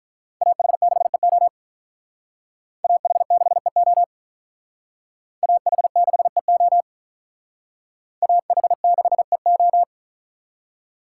Morse code, heard as a keyed tone, about 44 wpm, 700 Hz. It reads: AH6EO AH6EO AH6EO AH6EO